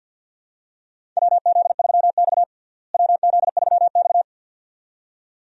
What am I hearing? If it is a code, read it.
Morse code, heard as a keyed tone, about 53 wpm, 700 Hz. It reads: WZ3X WZ3X
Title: USA Call Signs - 4 Characters - Repeated 2x - 50wpm